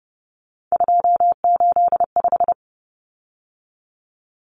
Morse code, heard as a keyed tone, about 30 wpm, 700 Hz. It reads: 285
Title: Common Cut Numbers - 3-Digit 30wpm